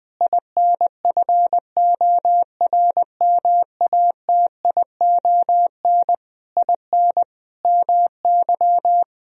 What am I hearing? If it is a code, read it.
Morse code, heard as a keyed tone, about 20 wpm, 700 Hz. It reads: INFORMATION IN MY
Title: Sentences from Top 1000 Words - Review Part 3 - 20wpm